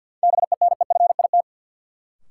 Morse code, heard as a keyed tone, about 52 wpm, 700 Hz. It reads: BENEFIT
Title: Sets of 5 Words Encore - New Word Drill - 50wpm